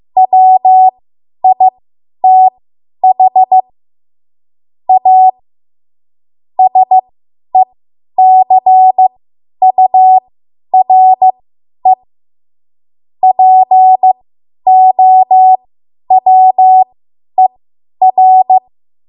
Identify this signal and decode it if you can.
Morse code, heard as a keyed tone, about 15 wpm, spaced as for 10 wpm, 750 Hz. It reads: WITH A SECURE POWER
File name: Archive / 10 WPM / 10WPM